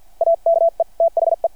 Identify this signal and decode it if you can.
Morse code, heard as a keyed tone, about 51 wpm, spaced as for 43 wpm, 660 Hz. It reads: AKE THE
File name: archived / CRSnet